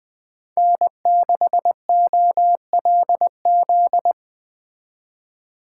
Morse code, heard as a keyed tone, about 20 wpm, 700 Hz. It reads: N6OLZ